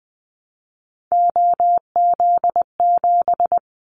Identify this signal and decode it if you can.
Morse code, heard as a keyed tone, about 20 wpm, 700 Hz. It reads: OZ7
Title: Callsign Prefixes for Top 100 DXCC - 20wpm